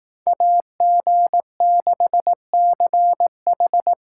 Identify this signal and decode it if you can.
Morse code, heard as a keyed tone, about 19 wpm, 700 Hz. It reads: AG6CH
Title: Speed Racing - USA Call Signs 15wpm